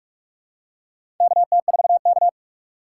Morse code, heard as a keyed tone, about 44 wpm, 690 Hz. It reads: KT4K
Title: USA Call Signs - 4 Characters - 3x Character Spacing - 45wpm